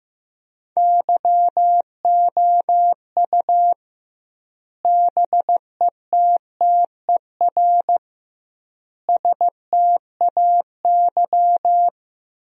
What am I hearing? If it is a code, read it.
Morse code, heard as a keyed tone, about 15 wpm, 700 Hz. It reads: YOU BETTER STAY
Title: 2x Word-Spacing - Sentences from Top 500 Words - 15wpm